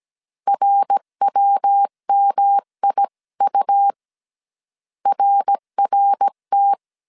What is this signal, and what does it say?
Morse code, heard as a keyed tone, about 17 wpm, 800 Hz. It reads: RWMIU RRT